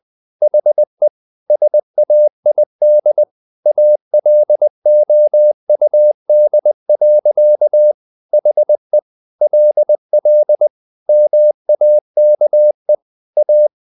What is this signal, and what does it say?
Morse code, heard as a keyed tone, about 20 wpm, 600 Hz. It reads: HE SAID ALOUD. HE LL MAKE A
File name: the old man and the sea / the old man and the sea part 1 20wpm